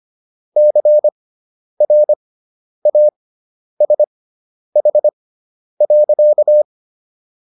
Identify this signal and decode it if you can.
Morse code, heard as a keyed tone, about 25 wpm, spaced as for 10 wpm, 600 Hz. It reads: CRASH.